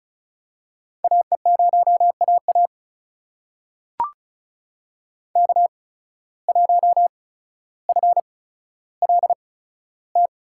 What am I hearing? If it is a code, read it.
Morse code, heard as a keyed tone, about 35 wpm, spaced as for 15 wpm, 700 Hz. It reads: AE0AA K1FLT
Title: USA Call Signs - 5 Characters - 8x Character Spacing - 35wpm